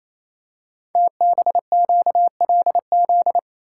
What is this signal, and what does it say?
Morse code, heard as a keyed tone, about 28 wpm, 690 Hz. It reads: TBQLZ
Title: Code Groups - 5 Letter:Numbers - 28wpm